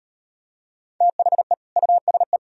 Morse code, heard as a keyed tone, about 38 wpm, 700 Hz. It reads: THE USE